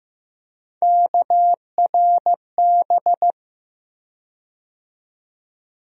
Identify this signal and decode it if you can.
Morse code, heard as a keyed tone, about 15 wpm, 700 Hz. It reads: KRB
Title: USA Call Sign Postfixes - 3 Letters 15wpm